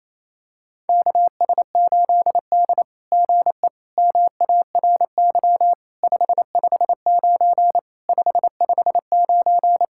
Morse code, heard as a keyed tone, about 28 wpm, 700 Hz. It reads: KS8D GE MARY 559 559
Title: SOTA QSO Practice 28wpm